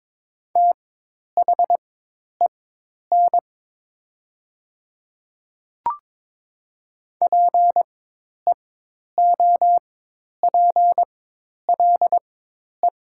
Morse code, heard as a keyed tone, about 22 wpm, spaced as for 10 wpm, 700 Hz. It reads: THEN PEOPLE
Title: 4x Character Spacing - Top 500 Words - 22wpm